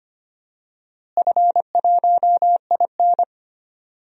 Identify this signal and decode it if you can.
Morse code, heard as a keyed tone, about 25 wpm, 700 Hz. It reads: F1IN
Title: DXCC-100 Call Signs - 4 Characters - Repeated 2x - 25wpm